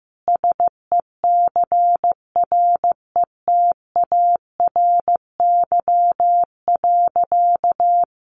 Morse code, heard as a keyed tone, about 15 wpm, 700 Hz. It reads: SECRETARY.
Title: Sentences from Top 1300 Words 15wpm